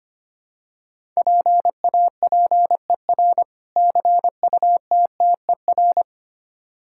Morse code, heard as a keyed tone, about 25 wpm, 700 Hz. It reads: PAPER CUTTER
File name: practice / 25wpm / Difficult Words with Hints - 25wpm